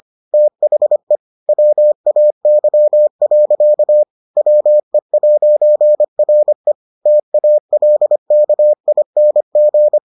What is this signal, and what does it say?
Morse code, heard as a keyed tone, about 25 wpm, 600 Hz. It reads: THE WAY. WE'RE TALKING